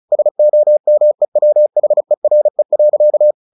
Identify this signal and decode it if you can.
Morse code, heard as a keyed tone, about 35 wpm, 600 Hz. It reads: SOMEWHERE.
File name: the old man and the sea / the old man and the sea 1 35wpm